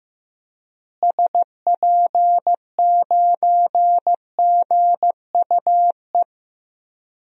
Morse code, heard as a keyed tone, about 15 wpm, 700 Hz. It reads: SP9GUE
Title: DXCC-100 Call Signs - 6 Characters - Repeated 3x - 15wpm